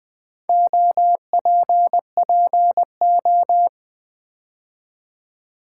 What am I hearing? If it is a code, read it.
Morse code, heard as a keyed tone, about 20 wpm, 700 Hz. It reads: OPPO
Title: Build A Word - 4 Letters At A Time - 11-Letter Words - 20wpm